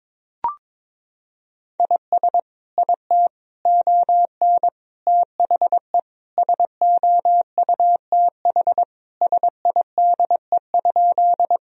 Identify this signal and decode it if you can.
Morse code, heard as a keyed tone, about 22 wpm, 700 Hz. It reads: IS IT ON THE SOUTH SIDE?
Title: Sentences from Top 1000 Words - Review Part 2 - 22wpm